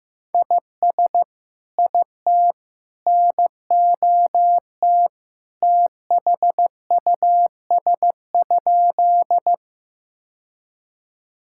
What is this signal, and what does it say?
Morse code, heard as a keyed tone, about 16 wpm, 700 Hz. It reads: IS IT NOT THUS?
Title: Sentences from Top 1000 Words - Review Part 5 - 15wpm